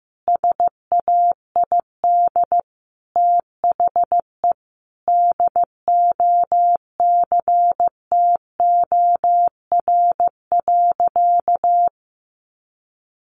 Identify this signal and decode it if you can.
Morse code, heard as a keyed tone, about 15 wpm, 700 Hz. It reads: SAID THE DOCTOR.
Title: Sentences from Top 1500 Words 15wpm